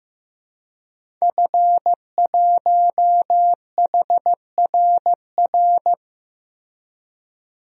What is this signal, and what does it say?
Morse code, heard as a keyed tone, about 15 wpm, 700 Hz. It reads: F1HRR
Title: DXCC-100 Call Signs - 5 Characters - Repeated 3x - 15wpm